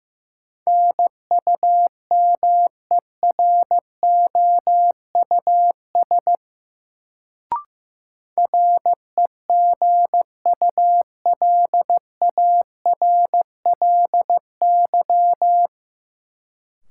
Morse code, heard as a keyed tone, about 15 wpm, 700 Hz. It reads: NUMEROUS REGULARLY
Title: Sentences from Top 1400 Words - New Word Drill Alt - 15wpm